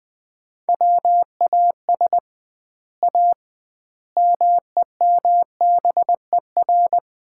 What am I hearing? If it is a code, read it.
Morse code, heard as a keyed tone, about 20 wpm, 700 Hz. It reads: WAS A MEMBER
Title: 2x Word-Spacing - Sets of 5 Words - 20wpm